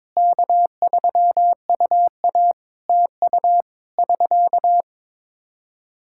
Morse code, heard as a keyed tone, about 22 wpm, 700 Hz. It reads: K3UA TU <SK>
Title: Practice QSO - Exchange 4 - Ending - 22wpm